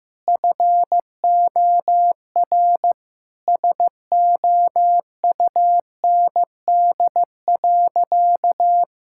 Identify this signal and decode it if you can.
Morse code, heard as a keyed tone, about 14 wpm, 700 Hz. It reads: FOR SOUND.
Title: Sentences from Top 1000 Words - Review Part 6 - 15wpm